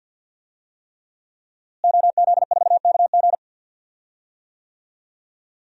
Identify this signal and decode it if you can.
Morse code, heard as a keyed tone, about 52 wpm, 700 Hz. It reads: OZ4KG
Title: DXCC-100 Call Signs - 5 Characters - Repeated 3x - 50wpm